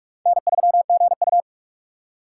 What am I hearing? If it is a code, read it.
Morse code, heard as a keyed tone, about 60 wpm, spaced as for 53 wpm, 700 Hz. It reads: N3GU
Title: Speed Racing - USA 1x2 Call Signs 45wpm